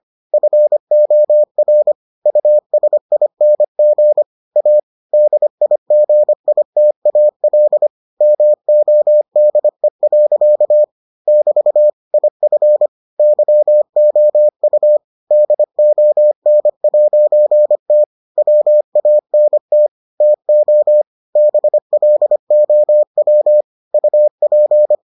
Morse code, heard as a keyed tone, about 25 wpm, 600 Hz. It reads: FOR USING A DIGITAL MODE. <BT> IF YOU DON'T WANT TO BLOW UP